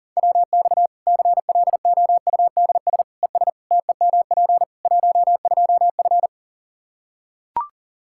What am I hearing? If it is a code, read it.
Morse code, heard as a keyed tone, about 40 wpm, 700 Hz. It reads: WX CLOUDS ES TEMP 12F
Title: K1USN SST QSO Practice 40wpm